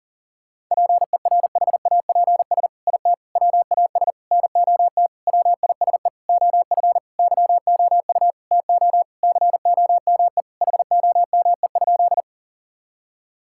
Morse code, heard as a keyed tone, about 37 wpm, 700 Hz. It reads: PERHAPS IT WAS NOT WISE OF YOU TO COME HOME?
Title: Sentences from Top 1600 Words 40wpm